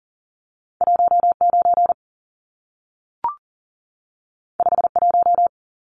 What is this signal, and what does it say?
Morse code, heard as a keyed tone, about 40 wpm, 700 Hz. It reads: 19 51
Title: Numbers - 2-Digit 40wpm